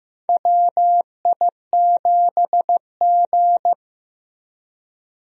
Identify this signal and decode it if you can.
Morse code, heard as a keyed tone, about 15 wpm, 700 Hz. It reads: WI7G
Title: USA Call Signs - 4 Characters - 15wpm